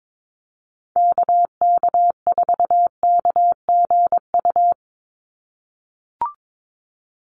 Morse code, heard as a keyed tone, about 22 wpm, 700 Hz. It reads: KK4KGU
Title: USA Call Signs 22wpm